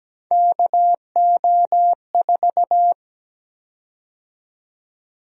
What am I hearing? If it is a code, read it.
Morse code, heard as a keyed tone, about 17 wpm, 700 Hz. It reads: KO4
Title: USA Call Sign Prefixes - 17wpm